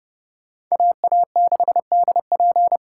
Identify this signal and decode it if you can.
Morse code, heard as a keyed tone, about 35 wpm, 700 Hz. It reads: AA6DP
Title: Speed Racing - USA 2x2 Call Signs 30wpm